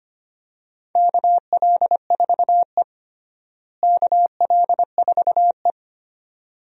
Morse code, heard as a keyed tone, about 25 wpm, 700 Hz. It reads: KL4E KL4E
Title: USA Call Signs - 4 Characters - Repeated 2x - 25wpm